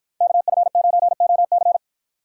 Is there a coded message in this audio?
Yes: Morse code from 0.2 to 1.8 s, heard as a keyed tone, about 51 wpm, 700 Hz. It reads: KF9OC